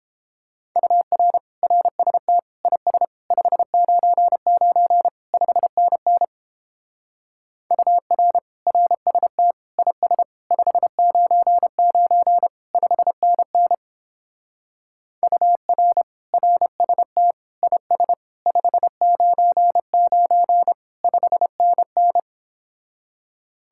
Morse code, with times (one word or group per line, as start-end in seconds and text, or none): 0.8-1.4 UR
1.6-2.4 RST
2.7-3.0 IS
3.3-5.1 599
5.3-6.2 5NN
7.7-8.4 UR
8.7-9.5 RST
9.8-10.2 IS
10.5-12.5 599
12.7-13.7 5NN
15.2-16.0 UR
16.3-17.3 RST
17.6-18.1 IS
18.5-20.7 599
21.0-22.2 5NN